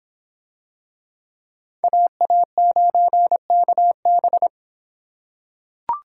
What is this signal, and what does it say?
Morse code, heard as a keyed tone, about 24 wpm, 700 Hz. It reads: AA9KB